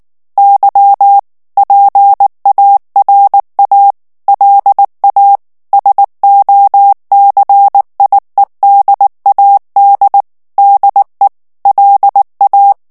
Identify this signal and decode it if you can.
Morse code, heard as a keyed tone, about 19 wpm, 800 Hz. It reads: Y PARA LA SOCIEDAD DE LA
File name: morse mp3 / ESPAGNOL TEXTE EN CLAIR 2 VITESSE 18 MOTS MINUTES